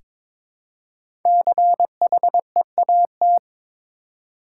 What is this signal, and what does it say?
Morse code, heard as a keyed tone, about 22 wpm, 700 Hz. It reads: CHEAT
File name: 5 Letter Words - First 3 Letters Given - 22wpm